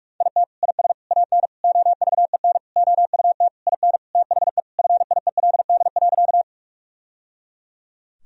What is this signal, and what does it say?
Morse code, heard as a keyed tone, about 41 wpm, 700 Hz. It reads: IT IS AN OVEN OUT IN THE FIELD.
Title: Sentences from Top 1000 Words - Review Part 4 - 45wpm